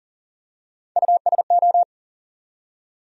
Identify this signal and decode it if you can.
Morse code, heard as a keyed tone, about 40 wpm, 700 Hz. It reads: USO
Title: USA Call Sign Postfixes - 3 Letters 40wpm